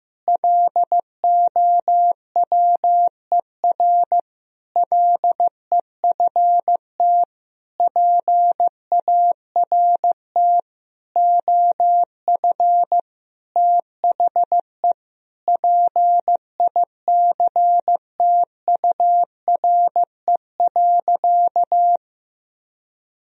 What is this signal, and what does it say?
Morse code, heard as a keyed tone, about 15 wpm, 700 Hz. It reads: LOWER LEFT PART OF THE PICTURE.
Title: Sentences from Top 1000 Words - Review Part 4 - 15wpm